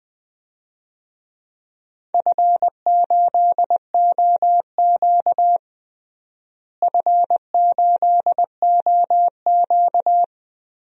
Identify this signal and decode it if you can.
Morse code, heard as a keyed tone, about 20 wpm, 700 Hz. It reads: F8OQ F8OQ
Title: DXCC-100 Call Signs - 4 Characters - Repeated 3x - 20wpm